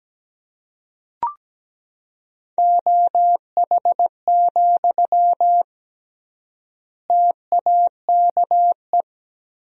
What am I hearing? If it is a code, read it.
Morse code, heard as a keyed tone, about 17 wpm, 700 Hz. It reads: OH, TAKE
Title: 3x Word-Spacing - Sentences from Top 500 Words - 17wpm